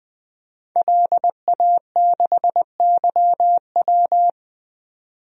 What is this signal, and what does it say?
Morse code, heard as a keyed tone, about 20 wpm, 700 Hz. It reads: LA6YW
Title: DXCC-100 Call Signs - 5 Characters - 20wpm